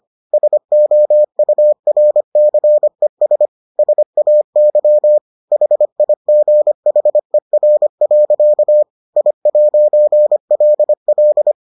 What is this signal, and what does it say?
Morse code, heard as a keyed tone, about 25 wpm, 600 Hz. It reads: SOURCES SAY HIGHER. I'LL